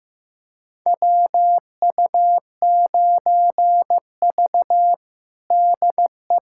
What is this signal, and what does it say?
Morse code, heard as a keyed tone, about 15 wpm, 700 Hz. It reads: WU9V DE
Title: USA Callsign DE Callsign - 4 Characters - 15wpm